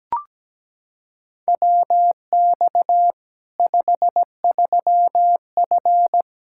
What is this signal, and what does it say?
Morse code, heard as a keyed tone, about 17 wpm, 700 Hz. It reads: WX 53F
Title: Practice QSO - Exchange 2 - Info - 17wpm